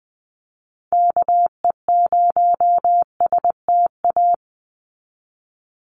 Morse code, heard as a keyed tone, about 20 wpm, 700 Hz. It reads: KE0STA
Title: USA Call Signs 20wpm